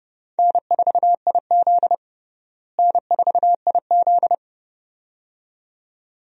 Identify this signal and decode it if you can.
Morse code, heard as a keyed tone, about 32 wpm, 700 Hz. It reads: N4IZ N4IZ
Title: USA Call Signs - 4 Characters - Repeated 2x - 30wpm